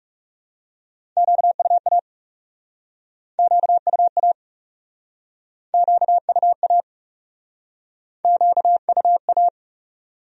Morse code, heard as a keyed tone, about 40 wpm, 700 Hz. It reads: QUA QUA QUA QUA